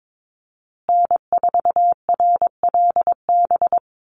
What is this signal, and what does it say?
Morse code, heard as a keyed tone, about 22 wpm, 700 Hz. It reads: N4RLB
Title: USA Call Signs Repeated 2x 22wpm